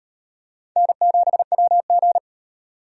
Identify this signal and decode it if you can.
Morse code, heard as a keyed tone, about 34 wpm, 700 Hz. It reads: N7WG